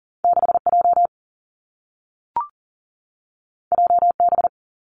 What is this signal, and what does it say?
Morse code, heard as a keyed tone, about 41 wpm, 700 Hz. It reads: 6J JB
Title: Two Challenging Letter-Number Combinations 40wpm